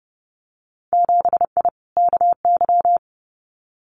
Morse code, heard as a keyed tone, about 30 wpm, 700 Hz. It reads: 7I KY
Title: Winter Field Day Exchanges Encore 30wpm